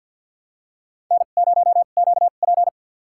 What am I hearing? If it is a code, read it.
Morse code, heard as a keyed tone, about 46 wpm, spaced as for 35 wpm, 700 Hz. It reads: N0QP
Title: USA Call Signs - 4 Characters - 2x Character Spacing - 50wpm